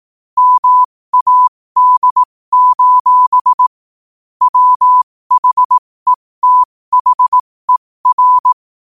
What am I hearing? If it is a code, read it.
Morse code, heard as a keyed tone, about 18 wpm, spaced as for 15 wpm, 1000 Hz. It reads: MAD: WHETHER
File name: qotdfiles / 15wpm